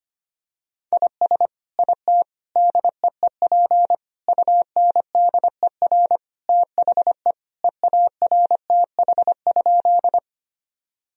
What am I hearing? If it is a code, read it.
Morse code, heard as a keyed tone, about 25 wpm, 700 Hz. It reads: IS IT DEEP UNDER THE EARTH?
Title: Sentences from Top 1000 Words - Review Part 3 - 25wpm